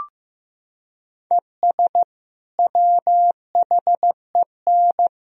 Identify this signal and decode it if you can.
Morse code, heard as a keyed tone, about 15 wpm, 700 Hz. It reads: ES WHEN
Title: Sentences from Top 100 Words and Abbreviations 15wpm